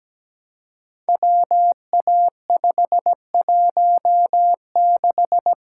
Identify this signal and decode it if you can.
Morse code, heard as a keyed tone, about 17 wpm, 700 Hz. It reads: WA516
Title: Code Groups - 5 Letter:Numbers - 17wpm